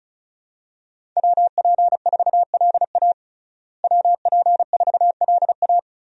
Morse code, heard as a keyed tone, about 36 wpm, 700 Hz. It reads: WP4LA WP4LA
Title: USA Call Signs - 5 Characters - Repeated 3x - 35wpm